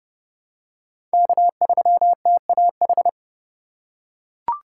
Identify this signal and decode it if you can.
Morse code, heard as a keyed tone, about 32 wpm, 700 Hz. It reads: K3TAH
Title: USA Call Signs - 5 Characters - 2x Character Spacing - 30wpm